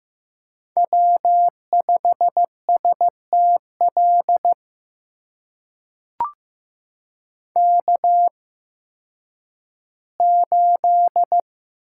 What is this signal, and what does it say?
Morse code, heard as a keyed tone, about 15 wpm, spaced as for 10 wpm, 700 Hz. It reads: W5STL K8
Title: USA Call Signs - 5 Characters - 8x Character Spacing - 15wpm